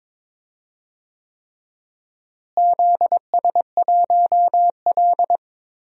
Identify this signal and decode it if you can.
Morse code, heard as a keyed tone, about 22 wpm, 700 Hz. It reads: ZS1L
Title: DXCC-100 Call Signs - 4 Characters - Repeated 3x - 22wpm